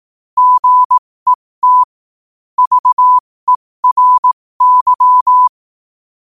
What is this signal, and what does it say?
Morse code, heard as a keyed tone, about 18 wpm, spaced as for 15 wpm, 1000 Hz. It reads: GET VERY
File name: qotdfiles / 15wpm